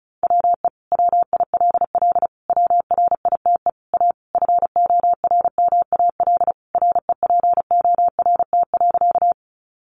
Morse code, heard as a keyed tone, about 34 wpm, 700 Hz. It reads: WE WILL WRITE A FORMAL REPORT.
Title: Sentences from Top 1300 Words 35wpm